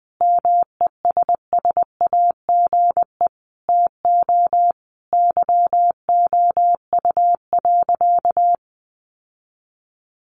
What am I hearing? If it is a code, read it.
Morse code, heard as a keyed tone, about 20 wpm, 700 Hz. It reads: MESSAGE TO YOU.
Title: Sentences from Top 1100 Words 20wpm